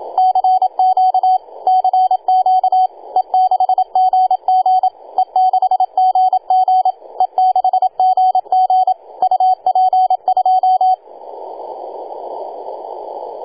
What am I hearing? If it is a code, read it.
Morse code, heard as a keyed tone, about 26 wpm, 750 Hz, with radio noise. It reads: CQ CQ E6GG E6GG E6GG UP2
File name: e6gg-18-083-cw